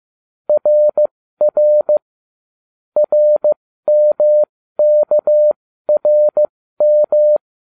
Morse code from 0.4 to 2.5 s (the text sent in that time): RR